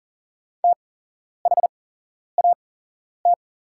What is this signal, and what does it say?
Morse code, heard as a keyed tone, about 40 wpm, spaced as for 11 wpm, 700 Hz. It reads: THAT
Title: 8x Character Spacing - Top 200 Words - 40wpm